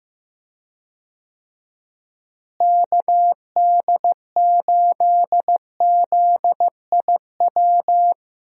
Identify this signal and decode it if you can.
Morse code, heard as a keyed tone, about 15 wpm, 700 Hz. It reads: KD8ZIW
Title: USA Call Signs - 6 Characters - Repeated 2x - 15wpm